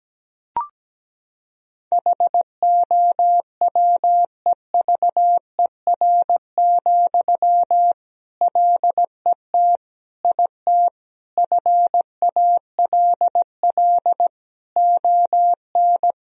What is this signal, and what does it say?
Morse code, heard as a keyed tone, about 17 wpm, 700 Hz. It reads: HOWEVER, LET IT FALL ON
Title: Sentences from Top 1000 Words - Review Part 2 - 17wpm